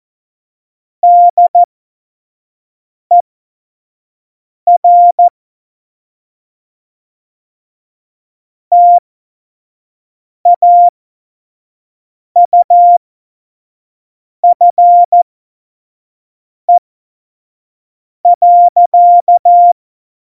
Morse code, heard as a keyed tone, about 14 wpm, spaced as for 5 wpm, 700 Hz. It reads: DER TAUFE.
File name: Morsekurs / Witze / Witz 06. Oktober 2020, 25 F70